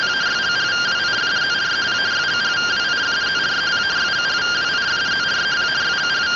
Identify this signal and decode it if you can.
Two-tone teleprinter signal, about 170 Hz shift.